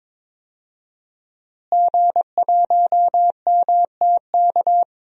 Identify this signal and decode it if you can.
Morse code, heard as a keyed tone, about 22 wpm, 700 Hz. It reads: G1MTK
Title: DXCC-100 Call Signs - 5 Characters - Repeated 2x - 22wpm